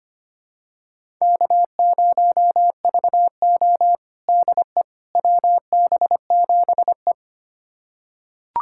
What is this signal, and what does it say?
Morse code, heard as a keyed tone, about 25 wpm, 700 Hz. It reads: K0VO DE WB7E
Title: USA Callsign DE Callsign - 4 Characters - 25wpm